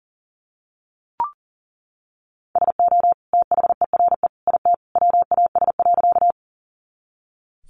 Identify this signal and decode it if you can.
Morse code, heard as a keyed tone, about 40 wpm, 700 Hz. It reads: SO THERE IT WAS.
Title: Sentences from Top 100 Words 40wpm